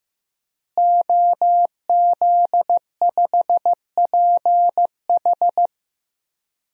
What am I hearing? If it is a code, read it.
Morse code, heard as a keyed tone, about 15 wpm, 700 Hz. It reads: OZ5PH
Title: DXCC-100 Call Signs - 5 Characters - Repeated 3x - 15wpm